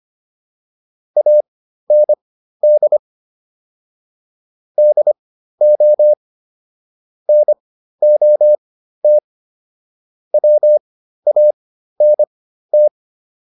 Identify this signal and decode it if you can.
Morse code, heard as a keyed tone, about 25 wpm, spaced as for 13 wpm, 600 Hz. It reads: AND DO NOT WANT
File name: WD404 13